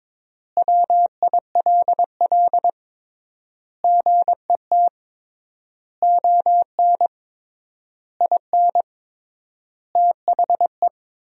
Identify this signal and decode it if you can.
Morse code, heard as a keyed tone, about 22 wpm, 700 Hz. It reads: WILL GET ON IN THE
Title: 3x Word-Spacing - Sentences from Top 200 Words - 22wpm